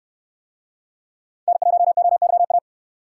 Morse code, heard as a keyed tone, about 69 wpm, 700 Hz. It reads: N1OYA